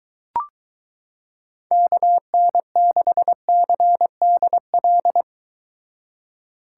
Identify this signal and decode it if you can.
Morse code, heard as a keyed tone, about 22 wpm, 700 Hz. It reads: KN6CDL